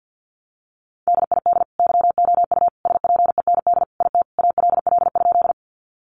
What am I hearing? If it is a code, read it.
Morse code, heard as a keyed tone, about 50 wpm, 700 Hz. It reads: DID YOU SPEND IT ALL?
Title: Sentences from Top 800 Words 50wpm